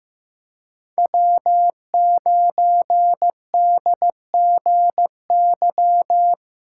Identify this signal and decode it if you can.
Morse code, heard as a keyed tone, about 15 wpm, 700 Hz. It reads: W9DGY
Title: USA Callsign DE Callsign - 5 Characters - 15wpm